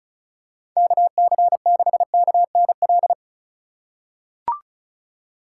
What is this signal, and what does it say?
Morse code, heard as a keyed tone, about 36 wpm, 700 Hz. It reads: KC6KNL
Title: USA Call Signs - 6 Characters - 2x Character Spacing - 35wpm